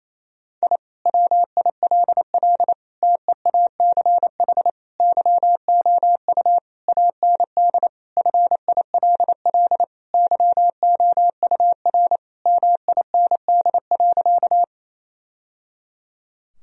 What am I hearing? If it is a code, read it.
Morse code, heard as a keyed tone, about 27 wpm, 700 Hz. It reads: I WILL TEACH YOU AND FILL YOUR MIND.
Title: Sentences from Top 1000 Words - Review Part 3 - 28wpm